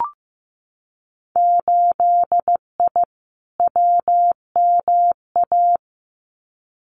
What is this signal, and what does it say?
Morse code, heard as a keyed tone, about 15 wpm, 700 Hz. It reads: 8I WMA
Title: Winter Field Day Exchanges Encore 15wpm